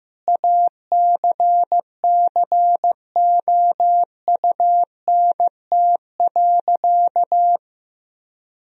Morse code, heard as a keyed tone, about 15 wpm, 700 Hz. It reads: ACCOUNT.
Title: Sentences from Top 1000 Words - Review Part 3 - 15wpm